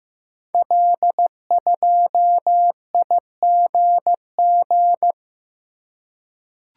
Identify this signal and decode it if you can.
Morse code, heard as a keyed tone, about 15 wpm, 700 Hz. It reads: L2IGG
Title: Code Groups - 5 Letter:Numbers - 15wpm